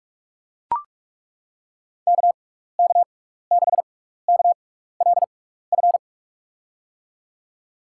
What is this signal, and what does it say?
Morse code, heard as a keyed tone, about 42 wpm, spaced as for 15 wpm, 700 Hz. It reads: KK6KLF
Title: USA Call Signs - 6 Characters - 6x Character Spacing - 45wpm